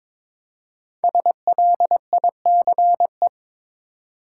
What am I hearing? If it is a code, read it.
Morse code, heard as a keyed tone, about 22 wpm, 700 Hz. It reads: SLICE
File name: 5 Letter Words - First 3 Letters Given - 22wpm